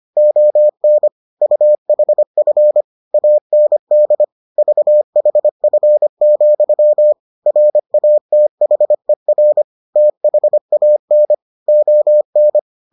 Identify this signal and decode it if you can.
Morse code, heard as a keyed tone, about 25 wpm, 600 Hz. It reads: ON UHF AND VHF, RATHER THAN ON